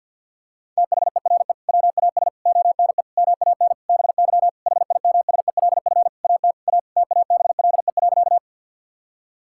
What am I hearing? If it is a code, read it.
Morse code, heard as a keyed tone, about 50 wpm, 700 Hz. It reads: THERE WAS ONE MAN BY HIMSELF AT A TABLE.
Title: Sentences from Top 1600 Words 50wpm